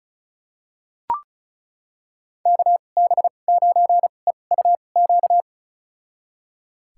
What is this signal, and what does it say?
Morse code, heard as a keyed tone, about 36 wpm, spaced as for 25 wpm, 700 Hz. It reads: KB9EUQ
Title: USA Call Signs - 6 Characters - 2x Character Spacing - 35wpm